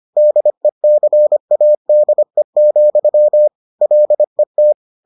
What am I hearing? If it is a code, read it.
Morse code, heard as a keyed tone, about 25 wpm, 600 Hz. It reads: DECADE, LET